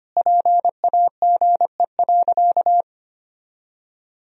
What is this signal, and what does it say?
Morse code, heard as a keyed tone, about 25 wpm, 700 Hz. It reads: PAGE.
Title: Sentences from Top 1000 Words - Review Part 1 - 25wpm